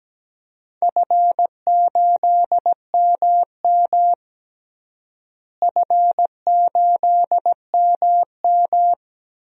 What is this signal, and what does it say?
Morse code, heard as a keyed tone, about 17 wpm, 700 Hz. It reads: F8MM F8MM
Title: DXCC-100 Call Signs - 4 Characters - Repeated 2x - 17wpm